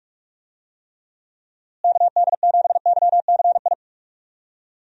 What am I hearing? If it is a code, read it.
Morse code, heard as a keyed tone, about 42 wpm, 700 Hz. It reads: KD7YCI
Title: USA Call Signs - 6 Characters - Repeated 2x - 45wpm